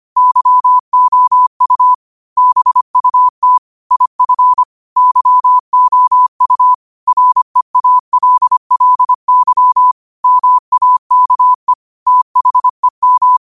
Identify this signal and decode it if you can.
Morse code, heard as a keyed tone, about 25 wpm, 1000 Hz. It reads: YOU BUT IF YOU REALLY MAKE THEM
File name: qotdfiles / 25wpm